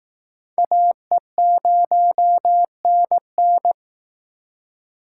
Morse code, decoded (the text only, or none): AE0NN